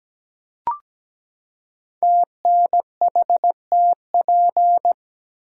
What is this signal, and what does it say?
Morse code, heard as a keyed tone, about 17 wpm, 700 Hz. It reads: TNHTP
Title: Code Groups - 5 Letters - 17wpm